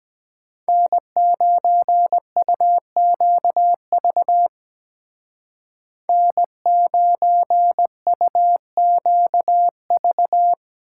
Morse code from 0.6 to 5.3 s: N9UQV

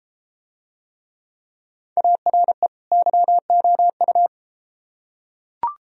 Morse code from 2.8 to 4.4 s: YOU

